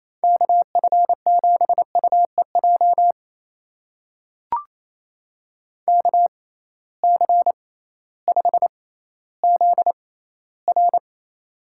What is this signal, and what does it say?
Morse code, heard as a keyed tone, about 28 wpm, spaced as for 15 wpm, 700 Hz. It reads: KF7UEJ KC5ZR
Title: USA Call Signs - 6x Character Spacing - 28wpm